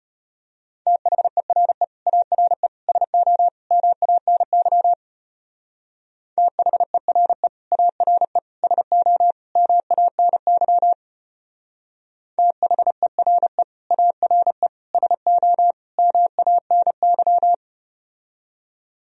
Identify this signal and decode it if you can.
Morse code, heard as a keyed tone, about 33 wpm, 700 Hz. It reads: THERE ARE SO MANY THERE ARE SO MANY THERE ARE SO MANY